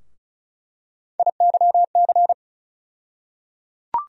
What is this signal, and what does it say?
Morse code, heard as a keyed tone, about 34 wpm, 700 Hz. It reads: IYC
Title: USA Call Sign Postfixes - 3 Letters 35wpm